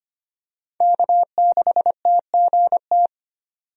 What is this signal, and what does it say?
Morse code, heard as a keyed tone, about 25 wpm, 700 Hz. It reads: K6TGT
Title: USA Call Signs - 5 Characters - 25wpm